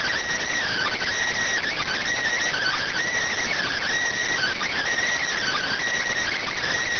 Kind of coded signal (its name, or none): Emergency Alert System (SAME) tones